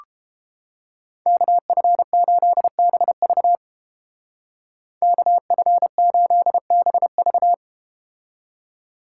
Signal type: Morse code

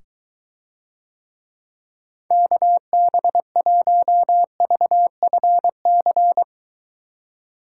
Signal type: Morse code

